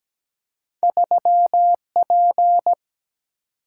Morse code, heard as a keyed tone, about 17 wpm, 700 Hz. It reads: 3P
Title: Single-Letter Single-Number 17wpm